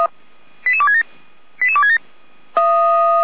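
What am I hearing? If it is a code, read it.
Five-tone selective call (ZVEI), heard as quick runs of five tones, tones of about 70 ms. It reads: 88267 88267
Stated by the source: ZVEI five-tone selective call received by SDR.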